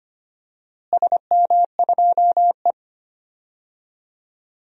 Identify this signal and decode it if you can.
Morse code, heard as a keyed tone, about 25 wpm, 700 Hz. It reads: SM2E
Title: DXCC-100 Call Signs - 4 Characters - Repeated 3x - 25wpm